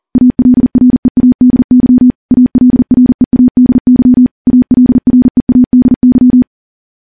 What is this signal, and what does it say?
Morse code, heard as a keyed tone, about 42 wpm, 250 Hz. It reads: ALREADY ALREADY ALREADY
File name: morse / morse-training-40WPM